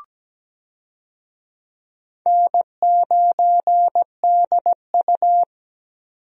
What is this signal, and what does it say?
Morse code, heard as a keyed tone, about 17 wpm, 700 Hz. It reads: N9DU
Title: USA Call Signs - 4 Characters - 17wpm